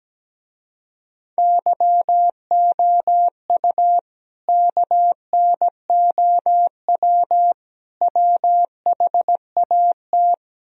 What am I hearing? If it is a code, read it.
Morse code, heard as a keyed tone, about 17 wpm, 700 Hz. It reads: YOU KNOW WHAT
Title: Speed-Racing - Sets of 3 Words 17wpm